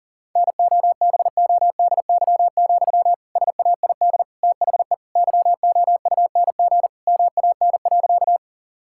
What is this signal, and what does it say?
Morse code, heard as a keyed tone, about 37 wpm, 700 Hz. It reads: NOBODY, SAID THE YOUNG MAN.
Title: Sentences from Top 1600 Words 40wpm